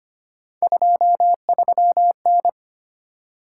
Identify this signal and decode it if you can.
Morse code, heard as a keyed tone, about 25 wpm, 700 Hz. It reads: 23N
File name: Three Letter-Number Combinations - 25wpm